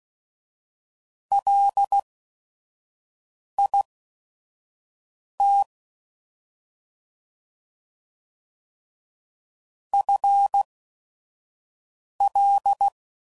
Morse code, heard as a keyed tone, about 16 wpm, spaced as for 5 wpm, 800 Hz. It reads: LIT FL
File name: Prefixes / FLs & THs / FL Prefixes 1